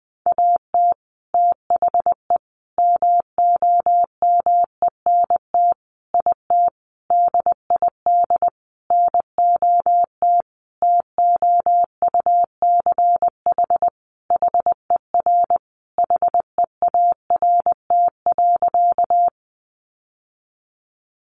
Morse code, heard as a keyed tone, about 20 wpm, 700 Hz. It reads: AT THE MOMENT IT DID NOT TOUCH HER HEART.
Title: Sentences from Top 1200 Words 20wpm